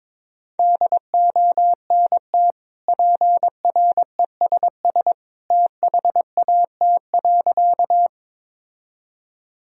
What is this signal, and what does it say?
Morse code, heard as a keyed tone, about 22 wpm, 700 Hz. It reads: DONT PRESS THAT.
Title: Sentences from Top 1000 Words - Review Part 1 - 22wpm